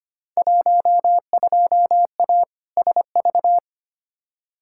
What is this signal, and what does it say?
Morse code, heard as a keyed tone, about 25 wpm, 700 Hz. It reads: 12A SV
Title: Field Day Exchanges Encore - 25wpm